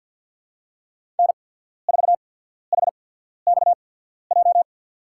Morse code, heard as a keyed tone, about 54 wpm, spaced as for 14 wpm, 700 Hz. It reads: N4HXJ
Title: USA Call Signs - 5 Characters - 8x Character Spacing - 50wpm